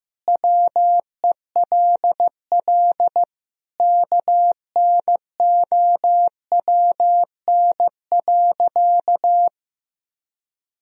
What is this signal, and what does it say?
Morse code, heard as a keyed tone, about 15 wpm, 700 Hz. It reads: WELL KNOWN.
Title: Sentences from Top 1000 Words - Review Part 3 - 15wpm